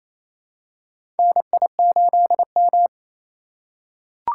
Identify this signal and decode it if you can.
Morse code, heard as a keyed tone, about 28 wpm, 700 Hz. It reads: NI8M
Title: USA Call Signs - 4 Characters - 3x Character Spacing - 28wpm